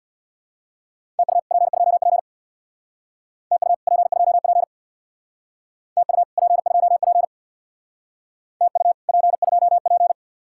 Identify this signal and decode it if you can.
Morse code, heard as a keyed tone, about 65 wpm, 700 Hz. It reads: TU P2P TU P2P TU P2P TU P2P